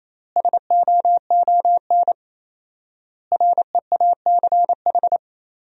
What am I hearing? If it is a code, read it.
Morse code, heard as a keyed tone, about 28 wpm, 700 Hz. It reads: SOON REACH
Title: 4x Word-Spacing - Sentences from Top 400 Words - 28wpm